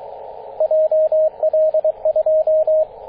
Morse code, heard as a keyed tone, about 22 wpm, 630 Hz, with radio noise. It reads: JL2
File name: 9m0o-24-891mhz-cw